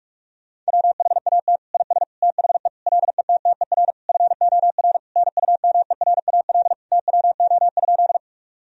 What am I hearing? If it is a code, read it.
Morse code, heard as a keyed tone, about 46 wpm, 700 Hz. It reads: WHAT IS THE LETTER FOR NUMERAL TWO?
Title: Sentences from Top 1000 Words - Review Part 2 - 45wpm